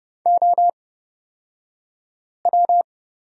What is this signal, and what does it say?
Morse code, heard as a keyed tone, about 31 wpm, spaced as for 5 wpm, 700 Hz. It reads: OW
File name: Lesson 38d - Letter q - Two Characters 30wpm